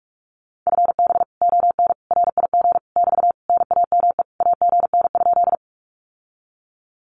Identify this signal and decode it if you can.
Morse code, heard as a keyed tone, about 46 wpm, 700 Hz. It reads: FB ON RIG <BT> NAME AGN?
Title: Common QSO Phrases Encore 45wpm